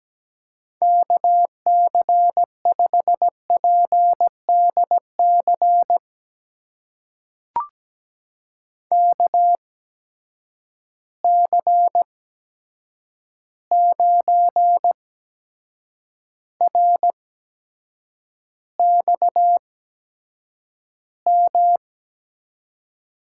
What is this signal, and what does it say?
Morse code, heard as a keyed tone, about 17 wpm, spaced as for 7 wpm, 700 Hz. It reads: KC5PDC KC9RXM
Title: USA Call Signs - 6 Characters - 8x Character Spacing - 17wpm